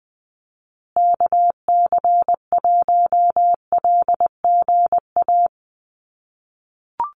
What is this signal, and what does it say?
Morse code, heard as a keyed tone, about 20 wpm, 700 Hz. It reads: KC1LGA
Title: USA Call Signs Repeated 3x 20wpm